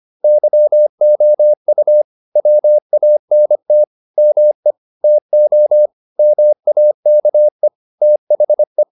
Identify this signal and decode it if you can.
Morse code, heard as a keyed tone, about 25 wpm, 600 Hz. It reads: YOU WANT ME TO MAKE THE
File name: the old man and the sea part 1 25wpm